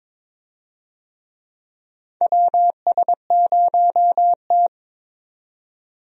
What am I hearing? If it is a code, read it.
Morse code, heard as a keyed tone, about 22 wpm, 700 Hz. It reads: WS0T
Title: USA Call Signs - 4 Characters - 22wpm